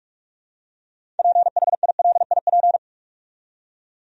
Morse code, heard as a keyed tone, about 46 wpm, 700 Hz. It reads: PHILIP
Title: USA Common Names - 45wpm